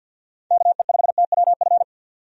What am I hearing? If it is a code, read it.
Morse code, heard as a keyed tone, about 45 wpm, 700 Hz. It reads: KE5TWF
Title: USA Call Signs - 6 Characters - 50wpm